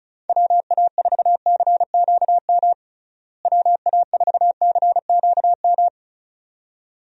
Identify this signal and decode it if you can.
Morse code, heard as a keyed tone, about 34 wpm, 700 Hz. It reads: WA4CQM WA4CQM
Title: USA Call Signs - 6 Characters - Repeated 2x - 35wpm